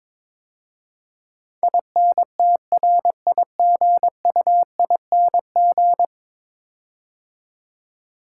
Morse code, heard as a keyed tone, about 21 wpm, 700 Hz. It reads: INTRIGUING
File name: practice / 22wpm / 10 Letter Words - First 3 Letters Given - 22wpm